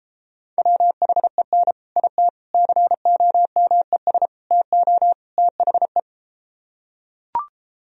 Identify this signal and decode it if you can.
Morse code, heard as a keyed tone, about 33 wpm, 700 Hz. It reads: WHEN IT COMES TO THE